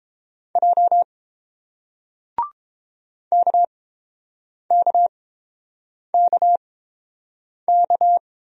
Morse code, heard as a keyed tone, about 29 wpm, spaced as for 7 wpm, 700 Hz. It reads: J KKKK